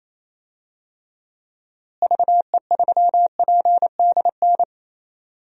Morse code, heard as a keyed tone, about 28 wpm, 700 Hz. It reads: VE3PDN
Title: DXCC-100 Call Signs - 6 Characters - Repeated 3x - 28wpm